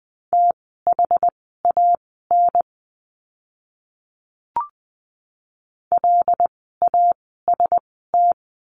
Morse code, heard as a keyed tone, about 20 wpm, spaced as for 15 wpm, 700 Hz. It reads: THAN LAST
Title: 146 Lesson 24f - Letter y - Words 20wpm